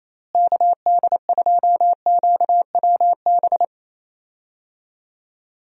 Morse code, heard as a keyed tone, about 28 wpm, 700 Hz. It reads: KD2QWB
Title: USA Call Signs - 6 Characters - Repeated 3x - 28wpm